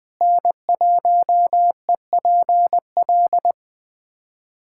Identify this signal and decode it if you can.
Morse code, heard as a keyed tone, about 20 wpm, 700 Hz. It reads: N1EPL
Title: USA Callsign DE Callsign - 5 Characters - 20wpm